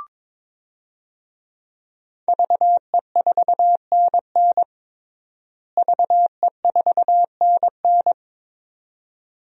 Morse code, heard as a keyed tone, about 22 wpm, 700 Hz. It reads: VE4NN VE4NN
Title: DXCC-100 Call Signs - 5 Characters - Repeated 2x - 22wpm